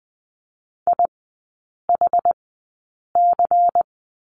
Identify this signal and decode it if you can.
Morse code, heard as a keyed tone, about 20 wpm, spaced as for 8 wpm, 700 Hz. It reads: IHC
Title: 100 Lesson 17e - Letter u - Three Characters 20wpm